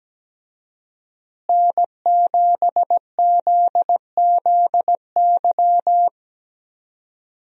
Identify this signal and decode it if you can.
Morse code, heard as a keyed tone, about 17 wpm, 700 Hz. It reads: N7ZZY
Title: USA Call Signs - 5 Characters - Repeated 3x - 17wpm